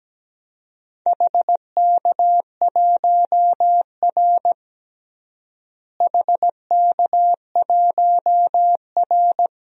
Morse code, heard as a keyed tone, about 17 wpm, 700 Hz. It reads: HK1R HK1R
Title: DXCC-100 Call Signs - 4 Characters - Repeated 3x - 17wpm